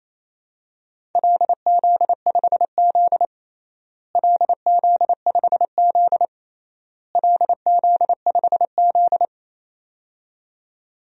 Morse code, heard as a keyed tone, about 28 wpm, 700 Hz. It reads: LZ5Z LZ5Z LZ5Z
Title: DXCC-100 Call Signs - 4 Characters - Repeated 3x - 28wpm